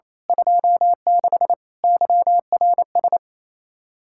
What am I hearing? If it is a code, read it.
Morse code, heard as a keyed tone, about 27 wpm, 700 Hz. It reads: 26 YRS
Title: Speed-Racing Common QSO Phrases - 28wpm